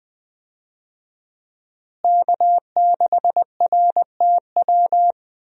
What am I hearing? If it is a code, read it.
Morse code, heard as a keyed tone, about 20 wpm, 700 Hz. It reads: K6RTW
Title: USA Call Signs - 5 Characters - Repeated 2x - 20wpm